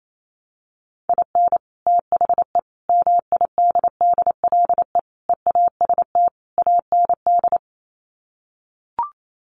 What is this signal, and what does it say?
Morse code, heard as a keyed tone, about 28 wpm, 700 Hz. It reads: IN THE MIDDLE EAST AND
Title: Sets of 5 Words Encore 28wpm